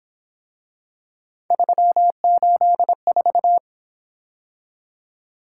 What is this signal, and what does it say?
Morse code, heard as a keyed tone, about 26 wpm, 700 Hz. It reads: 384
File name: Speed-Racing - Three Letter-Number Combinations - 22wpm